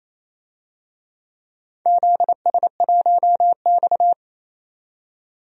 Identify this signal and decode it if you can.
Morse code, heard as a keyed tone, about 29 wpm, 700 Hz. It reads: ZS1X
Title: DXCC-100 Call Signs - 4 Characters - 28wpm